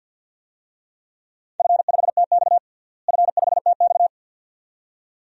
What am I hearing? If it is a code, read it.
Morse code, heard as a keyed tone, about 52 wpm, 700 Hz. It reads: F5TX F5TX
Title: DXCC-100 Call Signs - 4 Characters - Repeated 2x - 50wpm